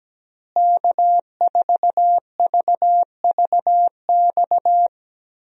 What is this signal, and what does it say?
Morse code, heard as a keyed tone, about 17 wpm, 700 Hz. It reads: K4VVX
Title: USA Call Signs - 5 Characters - Repeated 3x - 17wpm